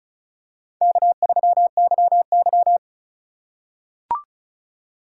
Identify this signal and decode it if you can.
Morse code, heard as a keyed tone, about 34 wpm, 700 Hz. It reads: K3YY
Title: USA Call Signs - 4 Characters - 4x Character Spacing - 35wpm